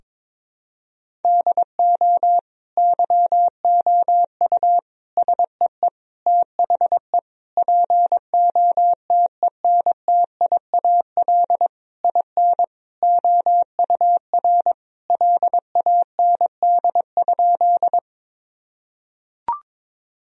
Morse code, heard as a keyed tone, about 22 wpm, 700 Hz. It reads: DO YOU SEE THE POTENTIAL IN OUR LAND?
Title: Sentences from Top 1000 Words - Review Part 4 - 22wpm